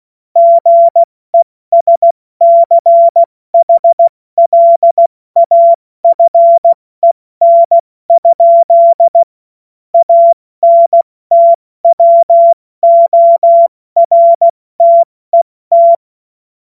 Morse code, heard as a keyed tone, about 16 wpm, 680 Hz. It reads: GESCHLAFEN? ANTWORTET
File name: Witz-05-August-2025-70-F80